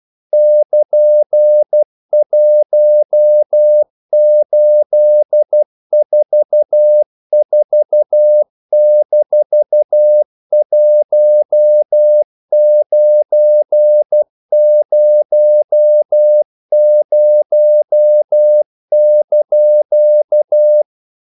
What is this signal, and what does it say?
Morse code, heard as a keyed tone, about 12 wpm, 600 Hz. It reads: <KN>1844-1900)
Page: English quotations in CW (telegraphy - morse)